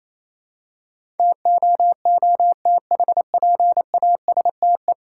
Morse code, heard as a keyed tone, about 27 wpm, 690 Hz. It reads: TOOTHPASTE
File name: Easy Words - Speed-Racing with Hints - 22wpm